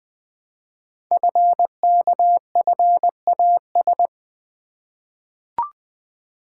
Morse code, heard as a keyed tone, about 20 wpm, 700 Hz. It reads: FKFAS
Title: Code Groups - 5 Letter:Numbers - 20wpm